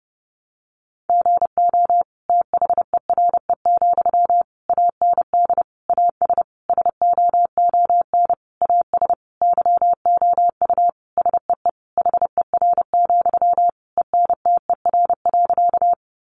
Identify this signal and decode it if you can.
Morse code, heard as a keyed tone, about 30 wpm, 700 Hz. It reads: GO THERE, AND AS SOON AS YOU SEE HER, ENTER.
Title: Sentences from Top 1300 Words 30wpm